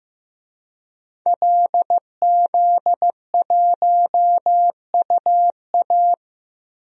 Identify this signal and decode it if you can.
Morse code, heard as a keyed tone, about 15 wpm, 700 Hz. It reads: LZ1UA
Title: DXCC-100 Call Signs - 5 Characters - 15wpm